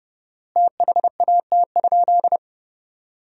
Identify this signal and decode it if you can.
Morse code, heard as a keyed tone, about 30 wpm, 700 Hz. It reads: THAT?
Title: 4x Word-Spacing - Sentences from Top 200 Words - 30wpm